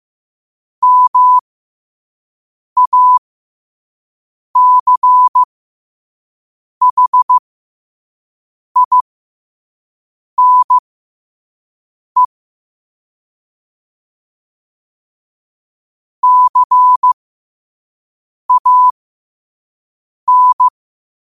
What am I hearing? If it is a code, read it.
Morse code, heard as a keyed tone, about 15 wpm, spaced as for 5 wpm, 1000 Hz. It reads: MACHINE CAN
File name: qotdfiles / 5wpm